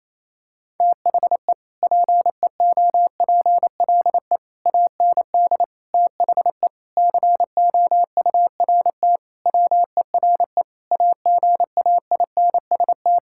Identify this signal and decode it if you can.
Morse code, heard as a keyed tone, about 28 wpm, 700 Hz. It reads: THE PEOPLE AND THE COURT WERE AGAINST
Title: Sentences from Top 1600 Words 28wpm